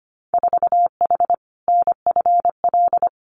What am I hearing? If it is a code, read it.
Morse code, heard as a keyed tone, about 25 wpm, 690 Hz. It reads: 4H NFL
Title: Winter Field Day Exchanges Encore 25wpm